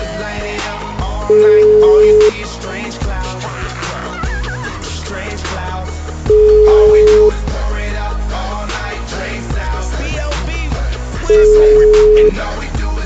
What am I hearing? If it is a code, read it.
Ringback tone (Europe, 425 Hz).